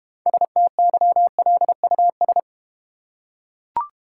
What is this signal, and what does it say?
Morse code, heard as a keyed tone, about 29 wpm, 700 Hz. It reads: STYLUS
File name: practice / 25wpm / Difficult Words - Speed-Racing with Hints - 25wpm